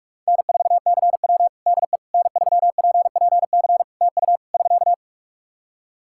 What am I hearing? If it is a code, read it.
Morse code, heard as a keyed tone, about 46 wpm, 700 Hz. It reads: N4CW DE N3PPC TU <SK>
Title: Practice QSO - Exchange 4 - Ending - 45wpm